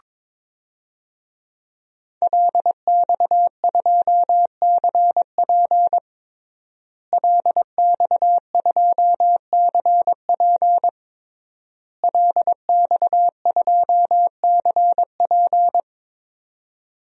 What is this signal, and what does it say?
Morse code, heard as a keyed tone, about 21 wpm, 700 Hz. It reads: LX2CP LX2CP LX2CP
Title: DXCC-100 Call Signs - 5 Characters - Repeated 3x - 22wpm